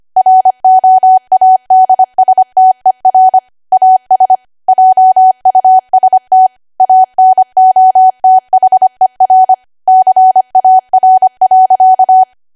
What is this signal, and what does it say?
Morse code, heard as a keyed tone, about 25 wpm, 750 Hz. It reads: ROADSTER AS JUST ANOTHER CAR.
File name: cwa / W1AW-Level3 / 25WPM